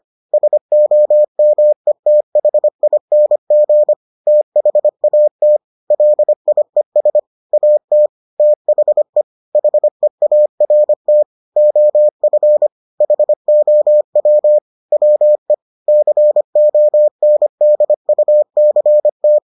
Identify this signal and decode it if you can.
Morse code, heard as a keyed tone, about 25 wpm, 600 Hz. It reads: SOMETHING THAT LIES AT THE HEART OF HOW WE CONDUCT